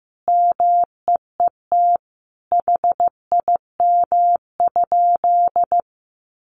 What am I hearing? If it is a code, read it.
Morse code, heard as a keyed tone, about 15 wpm, 700 Hz. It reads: MEET HIM?
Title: Sentences from Top 1100 Words 15wpm